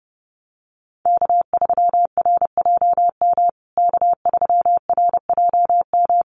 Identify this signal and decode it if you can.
Morse code, heard as a keyed tone, about 30 wpm, 700 Hz. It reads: K3RJM K3RJM
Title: USA Call Signs Repeated 2x 30wpm